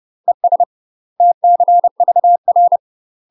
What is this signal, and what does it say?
Morse code, heard as a keyed tone, about 30 wpm, 700 Hz. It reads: ES TCVR
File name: UK MASTER QSOs TRAINING / 30WPM / UK MASTER QSOs 0018